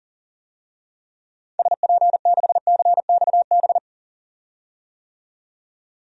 Morse code, heard as a keyed tone, about 40 wpm, 700 Hz. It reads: SP6CXB
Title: DXCC-100 Call Signs - 6 Characters - 40wpm